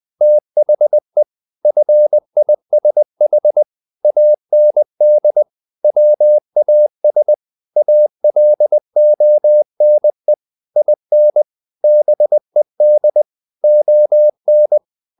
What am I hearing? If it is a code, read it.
Morse code, heard as a keyed tone, about 20 wpm, 600 Hz. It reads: THE FISH AND WAS ALONE IN BED ON
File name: the old man and the sea part 2 20wpm